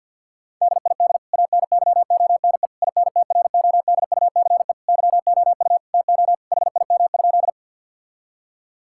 Morse code, heard as a keyed tone, about 51 wpm, 700 Hz. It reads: DID ANYONE INTRODUCE YOU TO HIM?
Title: Sentences from Top 1600 Words 50wpm